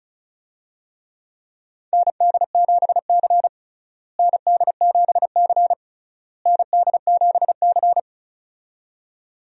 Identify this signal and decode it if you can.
Morse code, heard as a keyed tone, about 36 wpm, 700 Hz. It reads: ND7C ND7C ND7C
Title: USA Call Signs - 4 Characters - Repeated 3x - 35wpm